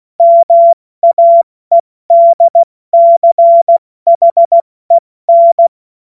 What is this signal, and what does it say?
Morse code, heard as a keyed tone, about 16 wpm, 680 Hz. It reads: MAEDCHEN
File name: Morsekurs / Witze / Witz 23. August 2022 70 F80